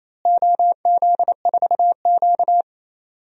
Morse code, heard as a keyed tone, about 28 wpm, 700 Hz. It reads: OZ4Q
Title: DXCC-100 Call Signs - 4 Characters - Repeated 2x - 28wpm